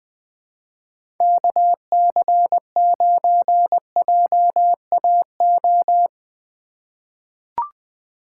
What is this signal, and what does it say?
Morse code, heard as a keyed tone, about 20 wpm, 700 Hz. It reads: KC9JAO
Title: USA Call Signs - 6 Characters - 4x Character Spacing - 20wpm